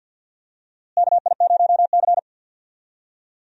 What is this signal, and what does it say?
Morse code, heard as a keyed tone, about 46 wpm, 700 Hz. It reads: KI0C
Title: USA Call Signs - 4 Characters - Repeated 2x - 50wpm